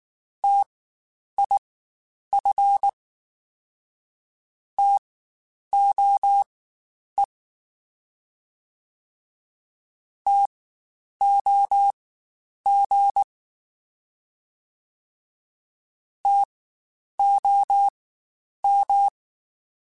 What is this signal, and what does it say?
Morse code, heard as a keyed tone, about 19 wpm, spaced as for 9 wpm, 800 Hz. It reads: TIF TOE TOG TOM
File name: Spaced Alphas / Spaced Alphas mp3s / Ts sp @18